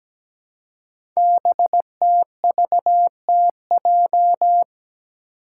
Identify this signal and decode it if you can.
Morse code, heard as a keyed tone, about 17 wpm, 700 Hz. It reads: BTVTJ
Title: Code Groups - 5 Letters - 17wpm